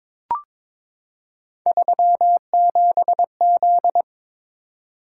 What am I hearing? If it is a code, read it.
Morse code, heard as a keyed tone, about 21 wpm, 700 Hz. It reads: 37Z
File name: practice / 22wpm / Three Letter-Number Combinations - 22wpm